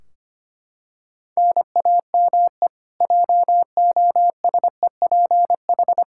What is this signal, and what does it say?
Morse code, heard as a keyed tone, about 25 wpm, 700 Hz. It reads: NAME JOSEPH
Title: Practice QSO - Exchange 1 - Beginning - 25wpm